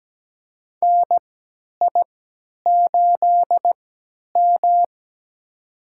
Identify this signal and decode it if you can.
Morse code, heard as a keyed tone, about 17 wpm, spaced as for 10 wpm, 700 Hz. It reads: NI8M
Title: USA Call Signs - 4 Characters - 3x Character Spacing - 17wpm